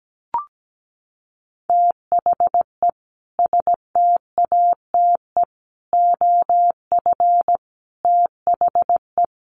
Morse code, heard as a keyed tone, about 17 wpm, 700 Hz. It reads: THE STATE OF THE
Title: Sentences from Top 1400 Words 17wpm